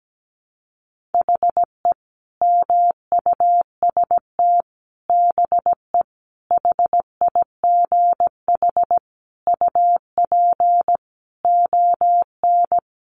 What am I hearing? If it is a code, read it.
Morse code, heard as a keyed tone, about 17 wpm, 700 Hz. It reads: HE MUST BE HIGH UP ON
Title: Sentences from Top 1200 Words 17wpm